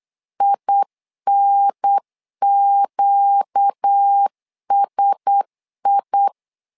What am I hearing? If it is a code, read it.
Morse code, heard as a keyed tone, about 8 wpm, 800 Hz. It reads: INQSI
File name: LEZIONI / 5 LBVC / 5 LB 8wpm 2816